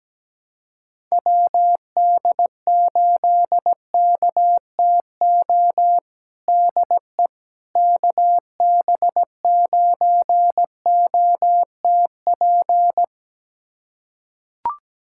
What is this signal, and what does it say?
Morse code, heard as a keyed tone, about 17 wpm, 700 Hz. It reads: WD8KTO DE KB9OTP
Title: USA Callsign DE Callsign - 6 Characters - 17wpm